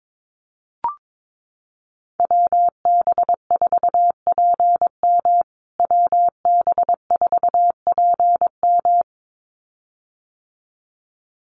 Morse code, heard as a keyed tone, about 22 wpm, 700 Hz. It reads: WB4PM WB4PM
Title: USA Call Signs Repeated 2x 22wpm